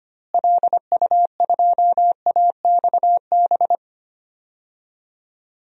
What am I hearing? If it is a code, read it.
Morse code, heard as a keyed tone, about 25 wpm, 700 Hz. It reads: LU2AXB
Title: DXCC-100 Call Signs - 6 Characters - Repeated 3x - 25wpm